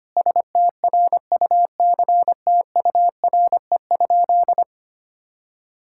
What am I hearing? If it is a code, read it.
Morse code, heard as a keyed tone, about 25 wpm, 700 Hz. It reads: STRUCTURE?
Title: Sentences from Top 1000 Words - Review Part 6 - 25wpm